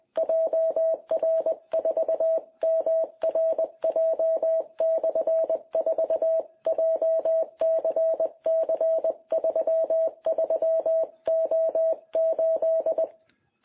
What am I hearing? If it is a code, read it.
Morse code, heard as a keyed tone, about 20 wpm, 660 Hz. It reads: JR4MRJ/4 JCC33O8
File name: jh8cbh / 2021cw